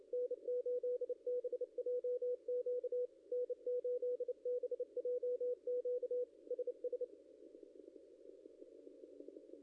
Morse code, heard as a keyed tone, about 29 wpm, 490 Hz, with radio noise. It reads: N8BJQ N8BJQ SS
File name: contests / ss-cw-k5tr-op-n5rz -0600